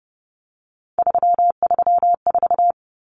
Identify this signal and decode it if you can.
Morse code, heard as a keyed tone, about 30 wpm, 700 Hz. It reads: 334
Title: Numbers - 3-Digit 30wpm